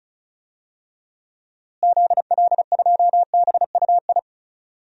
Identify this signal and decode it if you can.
Morse code, heard as a keyed tone, about 36 wpm, 700 Hz. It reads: ZL2BUI
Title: DXCC-100 Call Signs - 6 Characters - Repeated 2x - 35wpm